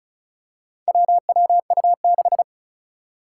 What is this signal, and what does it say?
Morse code, heard as a keyed tone, about 36 wpm, 700 Hz. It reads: WWU6
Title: Four Challenging Letter-Number Combinations 35wpm